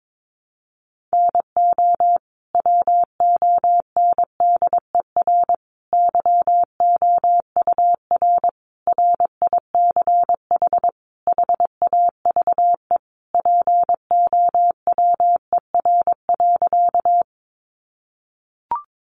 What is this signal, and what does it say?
Morse code, heard as a keyed tone, about 22 wpm, 700 Hz. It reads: NO WONDER YOUR RICH HAVE POWER.
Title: Sentences from Top 1100 Words 22wpm